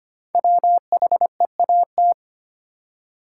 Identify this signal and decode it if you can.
Morse code, heard as a keyed tone, about 25 wpm, 700 Hz. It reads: WHEAT
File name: 5 Letter Words - First 3 Letters Given - 25wpm